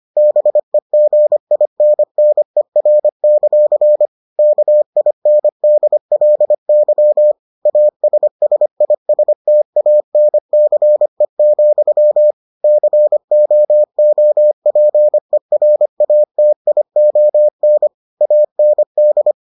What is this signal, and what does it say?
Morse code, heard as a keyed tone, about 25 wpm, 600 Hz. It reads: BEGINNER; KINDLY ASSISTANCE, COOPERATION AND